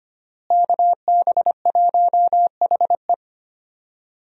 Morse code, heard as a keyed tone, about 25 wpm, 700 Hz. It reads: KB1HE
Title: USA Call Signs - 5 Characters - Repeated 3x - 25wpm